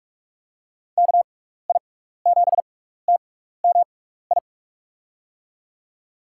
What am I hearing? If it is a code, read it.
Morse code, heard as a keyed tone, about 46 wpm, spaced as for 16 wpm, 700 Hz. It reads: KI7TMI
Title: USA Call Signs - 6 Characters - 6x Character Spacing - 45wpm